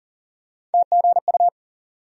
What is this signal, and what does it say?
Morse code, heard as a keyed tone, about 40 wpm, 700 Hz. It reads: TGU
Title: USA Call Sign Postfixes - 3 Letters 40wpm